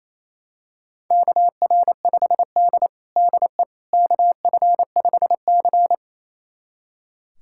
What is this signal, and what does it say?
Morse code, heard as a keyed tone, about 28 wpm, 700 Hz. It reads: KR5D DE KF5C
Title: USA Callsign DE Callsign - 4 Characters - 28wpm